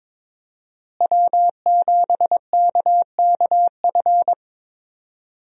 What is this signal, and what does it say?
Morse code, heard as a keyed tone, about 22 wpm, 700 Hz. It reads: W7KKF
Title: USA Call Signs - 5 Characters - Repeated 2x - 22wpm